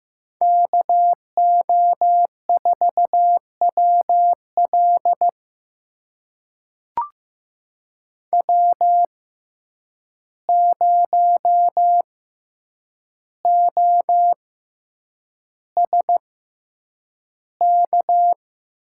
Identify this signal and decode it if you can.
Morse code, heard as a keyed tone, about 15 wpm, spaced as for 8 wpm, 700 Hz. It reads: KO4WL W0OSK
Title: USA Call Signs - 5 Characters - 6x Character Spacing - 15wpm